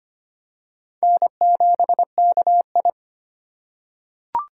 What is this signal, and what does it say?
Morse code, heard as a keyed tone, about 25 wpm, 700 Hz. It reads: N7KI
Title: USA Call Signs - 4 Characters - 2x Character Spacing - 25wpm